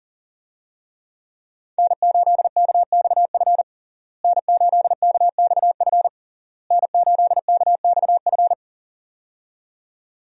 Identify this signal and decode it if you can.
Morse code, heard as a keyed tone, about 40 wpm, 700 Hz. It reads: N8KXF N8KXF N8KXF
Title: USA Call Signs - 5 Characters - Repeated 3x - 40wpm